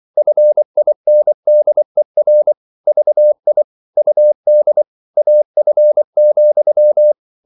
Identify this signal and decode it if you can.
Morse code, heard as a keyed tone, about 24 wpm, 600 Hz. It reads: FINDER VI UD AF,
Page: Ferie 24 wpm